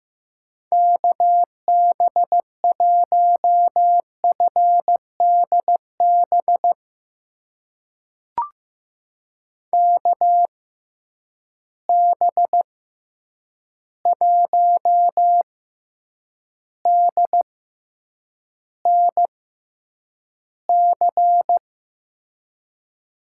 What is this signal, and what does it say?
Morse code, heard as a keyed tone, about 15 wpm, spaced as for 8 wpm, 700 Hz. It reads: KB1FDB KB1DNC
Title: USA Call Signs - 6 Characters - 6x Character Spacing - 15wpm